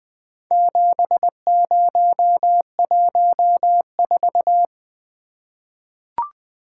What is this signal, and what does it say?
Morse code, heard as a keyed tone, about 20 wpm, 700 Hz. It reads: 7014
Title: Four Letter-Number Combinations - 20wpm